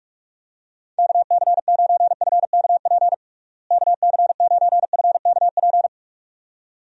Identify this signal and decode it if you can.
Morse code, heard as a keyed tone, about 46 wpm, 700 Hz. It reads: KC9FKP KC9FKP
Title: USA Call Signs - 6 Characters - Repeated 2x - 45wpm